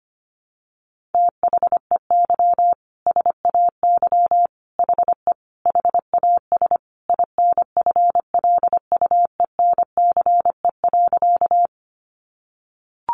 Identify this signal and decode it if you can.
Morse code, heard as a keyed tone, about 25 wpm, 700 Hz. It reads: THEY SAY HE HAS INFLUENCE.
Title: Sentences from Top 1100 Words 25wpm